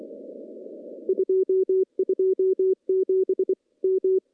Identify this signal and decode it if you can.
Morse code, heard as a keyed tone, about 23 wpm, 380 Hz. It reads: 227 M
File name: CW-SS-k5tr